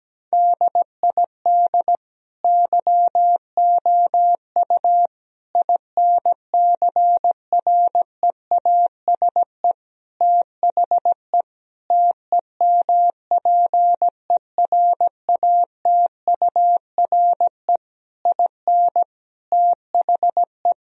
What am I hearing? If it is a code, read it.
Morse code, heard as a keyed tone, about 17 wpm, 700 Hz. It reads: DID YOU INCREASE THE TEMPERATURE IN THE
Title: Sentences from Top 1000 Words - Review Part 4 - 17wpm